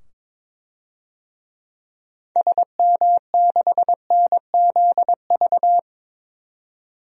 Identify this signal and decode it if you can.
Morse code, heard as a keyed tone, about 22 wpm, 700 Hz. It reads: SM6NZV
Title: DXCC-100 Call Signs - 6 Characters - 22wpm